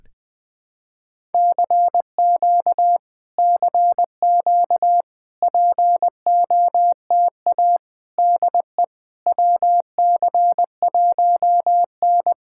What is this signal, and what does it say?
Morse code, heard as a keyed tone, about 20 wpm, 700 Hz. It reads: CQ CQ POTA DE WC1N
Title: POTA QSO Practice - 20wpm